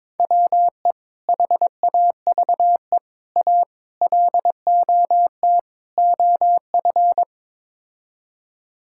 Morse code, heard as a keyed tone, about 22 wpm, 700 Hz. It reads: WE HAVE A LOT OF
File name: practice / 22wpm / Build A Phrase - 5-Word Phrases - 22wpm